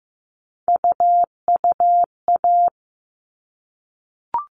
Morse code, heard as a keyed tone, about 15 wpm, 700 Hz. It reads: UUA
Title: All Cut Numbers - 3-Digit 15wpm